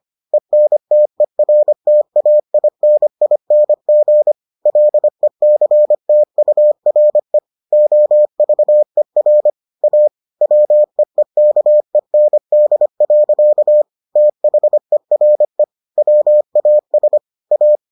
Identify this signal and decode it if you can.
Morse code, heard as a keyed tone, about 25 wpm, 600 Hz. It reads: ENTERTAINING LECTURE OVER A WEEKEND. THERE WAS A